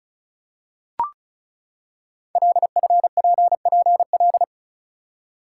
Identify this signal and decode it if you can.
Morse code, heard as a keyed tone, about 36 wpm, 690 Hz. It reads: LFPPL
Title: Code Groups - 5 Letters - 35wpm